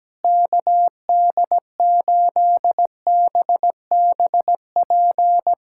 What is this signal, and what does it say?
Morse code, heard as a keyed tone, about 17 wpm, 700 Hz. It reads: KD8BBP
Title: USA Callsign DE Callsign 17wpm